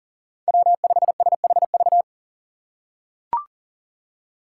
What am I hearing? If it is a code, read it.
Morse code, heard as a keyed tone, about 40 wpm, 700 Hz. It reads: W5SHV
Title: USA Call Signs - 2x Character Spacing - 40wpm